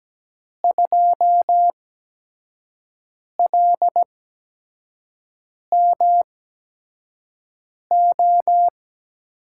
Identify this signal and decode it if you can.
Morse code, heard as a keyed tone, about 17 wpm, spaced as for 5 wpm, 700 Hz. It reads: 2LMO
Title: USA Call Signs - 5 Characters - 8x Character Spacing - 17wpm